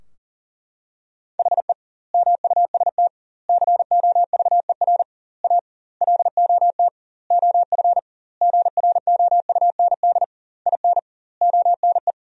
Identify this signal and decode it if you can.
Morse code, heard as a keyed tone, about 40 wpm, 700 Hz. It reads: HE MUST COVER A LOT OF GROUND IN ONE
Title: 2x Word-Spacing - Sentences from Top 500 Words - 40wpm